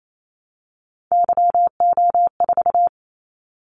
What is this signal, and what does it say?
Morse code, heard as a keyed tone, about 28 wpm, 700 Hz. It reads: YO4
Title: Callsign Prefixes for Top 25 DXCC - 28wpm